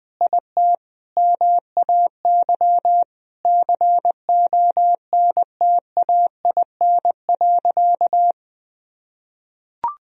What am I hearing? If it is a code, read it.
Morse code, heard as a keyed tone, about 21 wpm, 700 Hz. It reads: IT MAY CONTAIN.
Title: Sentences from Top 1000 Words - Review Part 2 - 20wpm